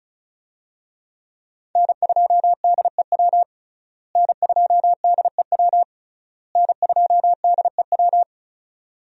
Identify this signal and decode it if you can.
Morse code, heard as a keyed tone, about 36 wpm, 700 Hz. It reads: N2DEW N2DEW N2DEW
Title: USA Call Signs - 5 Characters - Repeated 3x - 35wpm